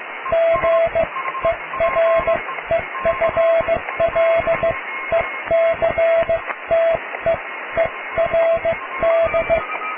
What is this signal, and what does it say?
Morse code, heard as a keyed tone, about 15 wpm, spaced as for 13 wpm, 660 Hz, with radio noise. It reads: GEREFLECTEERD
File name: cw / tekst(a)+QRM